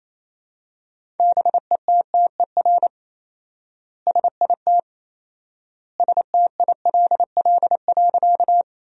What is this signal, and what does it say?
Morse code, heard as a keyed tone, about 27 wpm, 700 Hz. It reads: BETTER SIT STILL.
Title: 4x Word-Spacing - Sentences from Top 500 Words - 28wpm